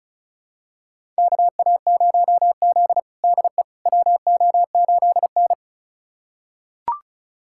Morse code, heard as a keyed tone, about 36 wpm, 700 Hz. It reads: KA0Z DE WO8N
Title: USA Callsign DE Callsign - 4 Characters - 35wpm